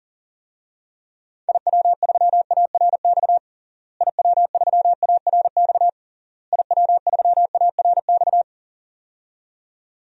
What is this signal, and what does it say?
Morse code, heard as a keyed tone, about 40 wpm, 700 Hz. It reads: IW3ARX IW3ARX IW3ARX
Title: DXCC-100 Call Signs - 6 Characters - Repeated 3x - 40wpm